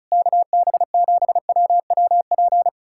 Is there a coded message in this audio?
Yes: Morse code from 0.1 to 2.7 s, heard as a keyed tone, about 40 wpm, 700 Hz. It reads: KB7WWP